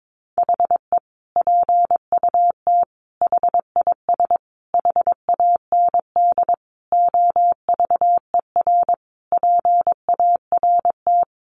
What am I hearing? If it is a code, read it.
Morse code, heard as a keyed tone, about 22 wpm, 700 Hz. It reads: HE PUT HIS HAND OVER PART
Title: Sentences from Top 400 Words 22wpm